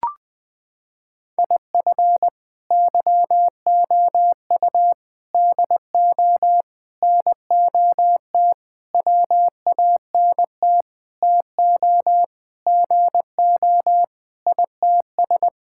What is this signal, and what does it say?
Morse code, heard as a keyed tone, about 21 wpm, 700 Hz. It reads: IF YOU DO NOT WANT TO GO ITS
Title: Sentences with 5-Word Phrases - 20wpm